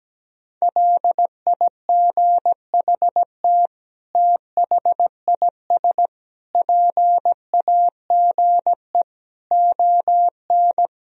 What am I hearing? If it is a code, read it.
Morse code, heard as a keyed tone, about 17 wpm, 700 Hz. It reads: LIGHT THIS PAGE ON
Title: Sentences from Top 1000 Words - Review Part 1 - 17wpm